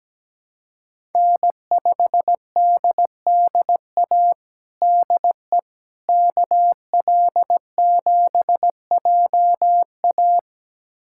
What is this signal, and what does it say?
Morse code, heard as a keyed tone, about 17 wpm, 700 Hz. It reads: N5DDA DE KL7JA
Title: USA Callsign DE Callsign - 5 Characters - 17wpm